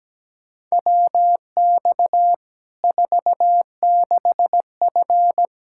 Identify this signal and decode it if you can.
Morse code, heard as a keyed tone, about 17 wpm, 700 Hz. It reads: WX 46F
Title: Practice QSO - Exchange 2 - Info - 17wpm